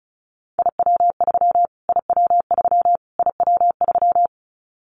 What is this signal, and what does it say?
Morse code, heard as a keyed tone, about 34 wpm, 700 Hz. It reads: IW3 IW3 IW3
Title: Callsign Prefixes for Top 25 DXCC - Repeated 3x - 35wpm